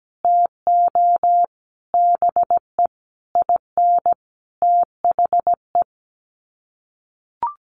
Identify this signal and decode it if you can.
Morse code, heard as a keyed tone, about 17 wpm, 700 Hz. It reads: TO BE IN THE
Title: Sets of 5 Words Encore 17wpm